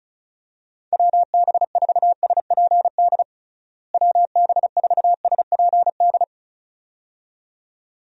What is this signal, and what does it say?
Morse code, heard as a keyed tone, about 34 wpm, 700 Hz. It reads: WB4SPD WB4SPD
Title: USA Call Signs - 6 Characters - Repeated 2x - 35wpm